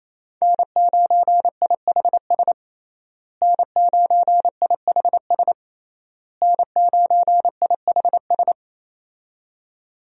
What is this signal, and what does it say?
Morse code, heard as a keyed tone, about 28 wpm, 700 Hz. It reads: N9IHS N9IHS N9IHS
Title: USA Call Signs - 5 Characters - Repeated 3x - 28wpm